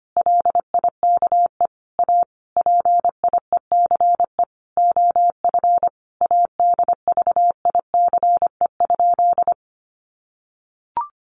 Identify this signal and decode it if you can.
Morse code, heard as a keyed tone, about 25 wpm, 700 Hz. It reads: LIKE A PIECE OF ADVICE?
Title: Sentences from Top 1100 Words 25wpm